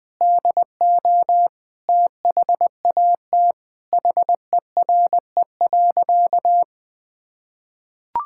Morse code, heard as a keyed tone, about 20 wpm, 700 Hz. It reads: DO THAT HERE.
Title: Sentences with 5-Word Phrases - Spoken Beginning - 20wpm